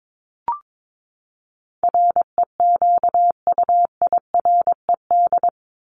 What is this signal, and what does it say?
Morse code, heard as a keyed tone, about 22 wpm, 700 Hz. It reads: REQUIRED
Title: ED Suffix - Top 100 Words with Base Words 22wpm